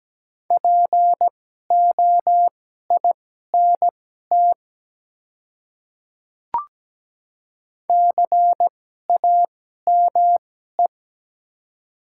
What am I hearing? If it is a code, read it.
Morse code, heard as a keyed tone, about 17 wpm, spaced as for 12 wpm, 700 Hz. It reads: POINT CAME
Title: 2x Character Spacing - Top 200 Words - 17wpm